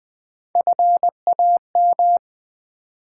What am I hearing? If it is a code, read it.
Morse code, heard as a keyed tone, about 20 wpm, 700 Hz. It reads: FAM
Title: Build A Word - 1 Letter At A Time - 4-Letter Words - 20wpm